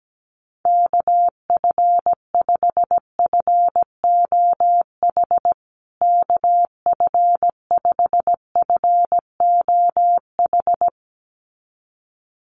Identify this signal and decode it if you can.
Morse code, heard as a keyed tone, about 17 wpm, 700 Hz. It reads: KF5FOH KF5FOH
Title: USA Call Signs Repeated 2x 17wpm